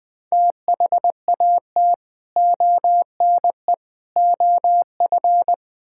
Morse code, heard as a keyed tone, about 20 wpm, 700 Hz. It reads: THAT ONE OF
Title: Build A Phrase - 4-Word Phrases - 20wpm